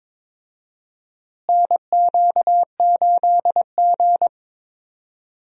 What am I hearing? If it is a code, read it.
Morse code, heard as a keyed tone, about 22 wpm, 700 Hz. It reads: NQ8G
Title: USA Call Signs - 4 Characters - Repeated 2x - 22wpm